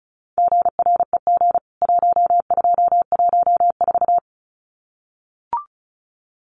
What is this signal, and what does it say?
Morse code, heard as a keyed tone, about 34 wpm, 700 Hz. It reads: GREG 1214
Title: CWT Exchange 35wpm